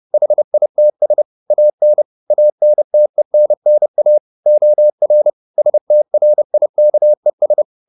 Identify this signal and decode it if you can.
Morse code, heard as a keyed tone, about 30 wpm, 600 Hz. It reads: HITS AN ANTENNA OR STRIKES